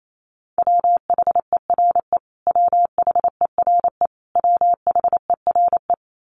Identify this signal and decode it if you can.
Morse code, heard as a keyed tone, about 28 wpm, 700 Hz. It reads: WHERE WHERE WHERE
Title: Top 500 Words Repeated 28wpm